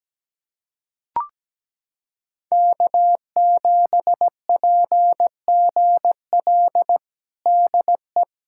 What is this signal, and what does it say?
Morse code, heard as a keyed tone, about 17 wpm, 700 Hz. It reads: K7PGL DE
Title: USA Callsign DE Callsign - 5 Characters - 17wpm